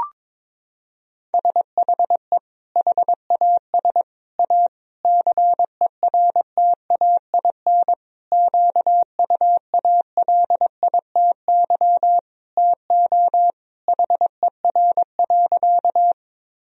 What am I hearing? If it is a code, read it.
Morse code, heard as a keyed tone, about 22 wpm, 700 Hz. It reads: SHE HAS A CERTAIN QUALITY TO HER.
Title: Sentences from Top 1000 Words - Review Part 4 - 22wpm